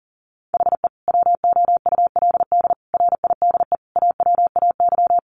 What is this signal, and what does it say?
Morse code, heard as a keyed tone, about 40 wpm, 700 Hz. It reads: HE WOULD RIDE AWAY
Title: Sentences from Top 1200 Words 40wpm